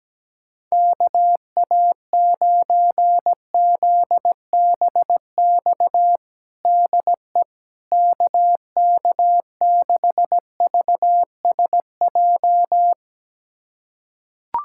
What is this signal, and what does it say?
Morse code, heard as a keyed tone, about 17 wpm, 700 Hz. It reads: KA9ZBX DE KK6VSJ
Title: USA Callsign DE Callsign - 6 Characters - 17wpm